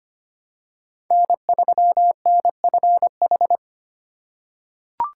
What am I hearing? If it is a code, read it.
Morse code, heard as a keyed tone, about 25 wpm, 700 Hz. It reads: N3NFH
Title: USA Call Signs - 5 Characters - 3x Character Spacing - 25wpm